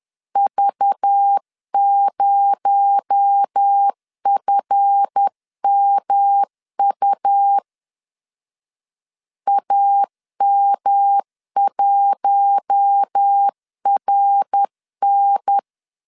Morse code, heard as a keyed tone, about 10 wpm, 800 Hz. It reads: V0FMU AM1RN
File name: LEZIONI / 7 GF9Z / 7 GF 10wpm 2514